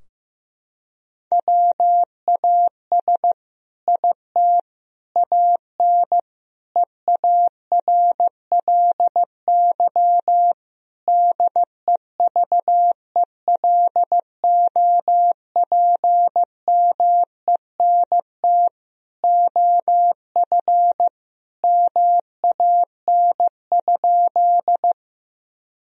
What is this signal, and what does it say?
Morse code, heard as a keyed tone, about 15 wpm, 700 Hz. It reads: WAS IT AN EARLY DEVELOPMENT OF MAN?
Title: Sentences from Top 1000 Words - Review Part 4 - 15wpm